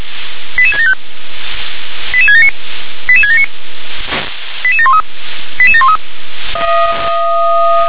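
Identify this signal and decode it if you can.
Five-tone selective call (ZVEI), heard as quick runs of five tones, tones of about 80 ms. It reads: 88675 88679 88679 88713 88713
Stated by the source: ZVEI five-tone selective call 88776 received on SDR.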